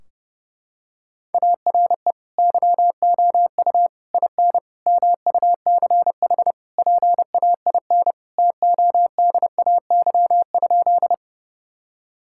Morse code, heard as a keyed tone, about 30 wpm, 700 Hz. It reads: ARE YOU IN MUCH PAIN TODAY?
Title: Sentences from Top 1600 Words 30wpm